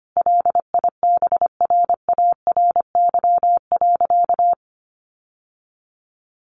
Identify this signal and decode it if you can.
Morse code, heard as a keyed tone, about 25 wpm, 700 Hz. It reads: LIBRARY.
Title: Sentences from Top 600 Words 25wpm